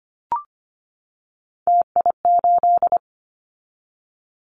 Morse code, heard as a keyed tone, about 25 wpm, 700 Hz. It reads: TI8
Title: Callsign Prefixes for Top 100 DXCC - 25wpm